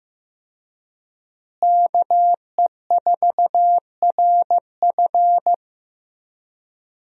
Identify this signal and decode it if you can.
Morse code, heard as a keyed tone, about 15 wpm, 700 Hz. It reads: KE4RF
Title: USA Call Signs - 5 Characters - Repeated 3x - 15wpm